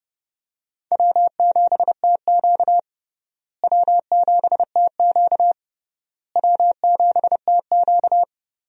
Morse code, heard as a keyed tone, about 32 wpm, 700 Hz. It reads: W7TQ W7TQ W7TQ
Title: USA Call Signs - 4 Characters - Repeated 3x - 30wpm